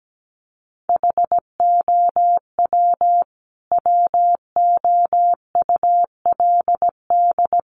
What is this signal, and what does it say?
Morse code, heard as a keyed tone, about 17 wpm, 700 Hz. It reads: HOW WOULD
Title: Sentences from Top 100 Words 17wpm